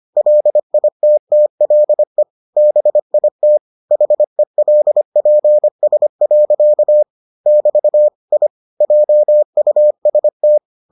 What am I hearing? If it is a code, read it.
Morse code, heard as a keyed tone, about 25 wpm, 600 Hz. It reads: LITTLE BIT HELPS. <BT> I JUST